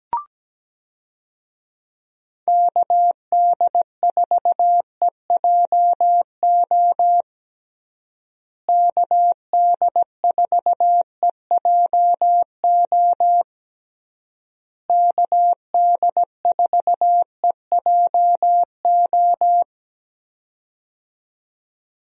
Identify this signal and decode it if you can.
Morse code, heard as a keyed tone, about 17 wpm, 700 Hz. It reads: KD4EJO KD4EJO KD4EJO
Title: USA Call Signs - 6 Characters - Repeated 3x - 17wpm